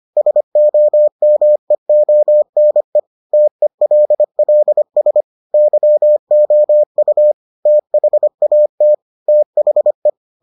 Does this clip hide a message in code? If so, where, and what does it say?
Yes: Morse code from 0.2 to 10.1 s, heard as a keyed tone, about 25 wpm, 600 Hz. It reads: SOMEONE TELLS YOU THAT THE